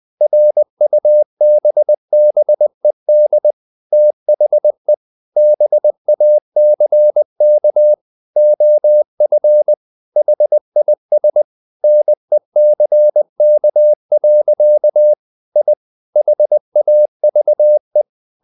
Morse code, heard as a keyed tone, about 20 wpm, 600 Hz. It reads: RUBBED THE BACK OF HIS NECK. I HAVE
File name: the old man and the sea part 2 20wpm